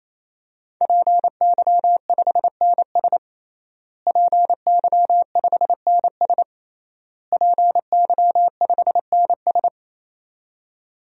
Morse code, heard as a keyed tone, about 27 wpm, 700 Hz. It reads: PY5NS PY5NS PY5NS
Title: DXCC-100 Call Signs - 5 Characters - Repeated 3x - 28wpm